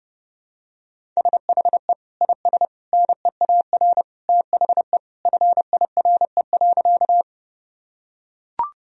Morse code, heard as a keyed tone, about 32 wpm, 700 Hz. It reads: SHE IS NEAR THE FIRE.
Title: Sentences from Top 1000 Words - Review Part 1 - 30wpm